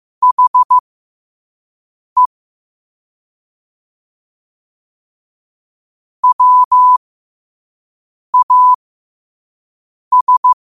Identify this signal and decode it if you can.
Morse code, heard as a keyed tone, about 15 wpm, spaced as for 5 wpm, 1000 Hz. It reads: HE WAS
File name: qotdfiles / 5wpm